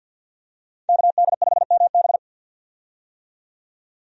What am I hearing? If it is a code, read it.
Morse code, heard as a keyed tone, about 53 wpm, 700 Hz. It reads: KD5MB
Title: USA Call Signs - 5 Characters - 50wpm